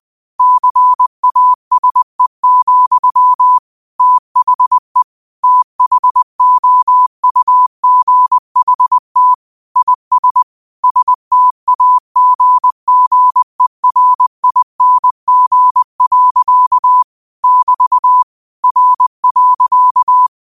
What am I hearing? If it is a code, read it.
Morse code, heard as a keyed tone, about 20 wpm, 1000 Hz. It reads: CASE, THE THOUGHT IS STAGGERING. <BT> R.
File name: qotdfiles / 20wpm